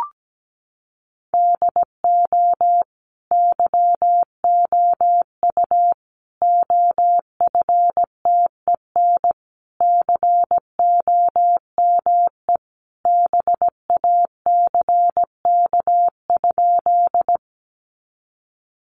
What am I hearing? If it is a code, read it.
Morse code, heard as a keyed tone, about 17 wpm, 700 Hz. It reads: DO YOU OFTEN COME BACK?
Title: Sentences from Top 300 Words 17wpm